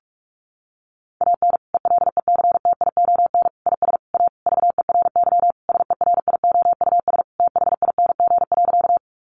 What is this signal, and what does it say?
Morse code, heard as a keyed tone, about 44 wpm, 700 Hz. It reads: AN ELECTION IS A VERY SERIOUS THING.
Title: Sentences from Top 1500 Words 45wpm